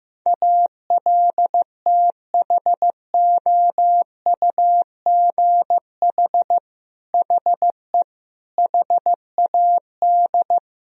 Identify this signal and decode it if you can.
Morse code, heard as a keyed tone, about 15 wpm, 700 Hz. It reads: ALTHOUGH HE HAD
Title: Sentences from Top 1000 Words - Review Part 3 - 15wpm